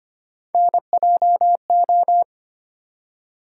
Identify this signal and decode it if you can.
Morse code, heard as a keyed tone, about 25 wpm, 700 Hz. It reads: NJO
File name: Three Letter-Number Combinations - 25wpm